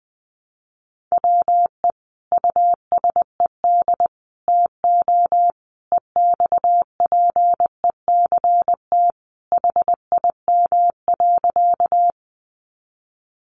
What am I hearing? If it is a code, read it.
Morse code, heard as a keyed tone, about 20 wpm, 700 Hz. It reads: WE USED TO EXPECT HIM.
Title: Sentences from Top 1100 Words 20wpm